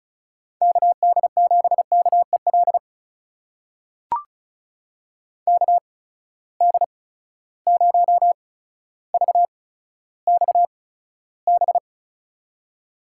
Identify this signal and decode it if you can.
Morse code, heard as a keyed tone, about 36 wpm, spaced as for 15 wpm, 700 Hz. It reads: KD7KEL KD0VXB
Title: USA Call Signs - 6 Characters - 8x Character Spacing - 35wpm